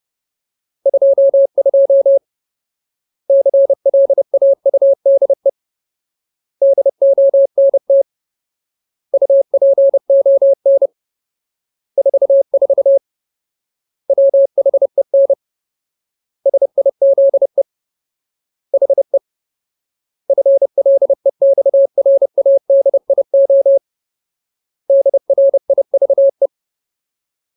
Morse code, heard as a keyed tone, about 30 wpm, 550 Hz. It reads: 22 CLAUDE DONT UPON 44 WHEN SIZE HE FLEXRADIO DRIVE
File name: Ultimate Word List / 30WPM Ultimate